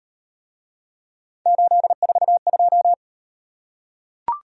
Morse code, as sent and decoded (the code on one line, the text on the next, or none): ---.. ....- ..---
842